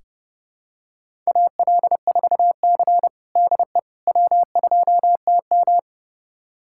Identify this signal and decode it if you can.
Morse code, heard as a keyed tone, about 30 wpm, 700 Hz. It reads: AL4C DE W2TM
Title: USA Callsign DE Callsign - 4 Characters - 30wpm